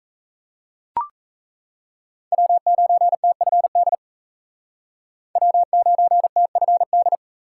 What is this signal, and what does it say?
Morse code, heard as a keyed tone, about 35 wpm, 700 Hz. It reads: W9TFD W9TFD
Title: Speed Racing - USA 1x3 Call Signs 28wpm